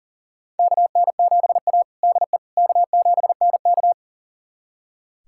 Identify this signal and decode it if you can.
Morse code, heard as a keyed tone, about 40 wpm, 700 Hz. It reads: KN7A DE K7NK
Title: USA Callsign DE Callsign - 4 Characters - 40wpm